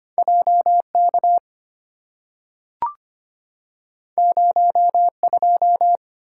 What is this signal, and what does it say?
Morse code, heard as a keyed tone, about 25 wpm, 700 Hz. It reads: JK 02
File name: Two Letter-Number Combinations - 25wpm